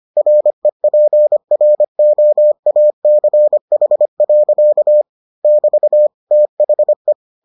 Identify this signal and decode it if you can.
Morse code, heard as a keyed tone, about 25 wpm, 600 Hz. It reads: REPROACH. <BT> THE